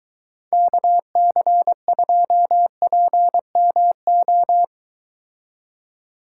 Morse code, heard as a keyed tone, about 22 wpm, 700 Hz. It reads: KC2PMO